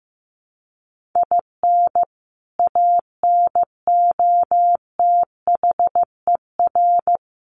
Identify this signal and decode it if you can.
Morse code, heard as a keyed tone, about 15 wpm, 700 Hz. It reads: IN ANOTHER
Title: Sentences from Top 1500 Words 15wpm